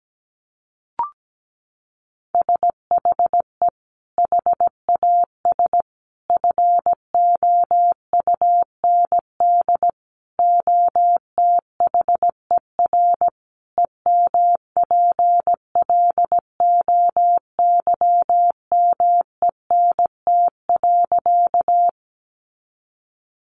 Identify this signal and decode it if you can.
Morse code, heard as a keyed tone, about 17 wpm, 700 Hz. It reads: SHE HAS FOUND OTHER EMPLOYMENT.
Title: Sentences from Top 1500 Words 17wpm